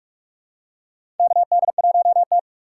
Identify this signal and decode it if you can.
Morse code, heard as a keyed tone, about 46 wpm, 700 Hz. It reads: KD1T
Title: USA Call Signs - 4 Characters - 2x Character Spacing - 45wpm